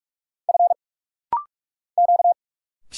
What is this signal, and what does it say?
Morse code, heard as a keyed tone, about 46 wpm, 700 Hz. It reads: F Q
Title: Single Letters - Rapid-Fire 45wpm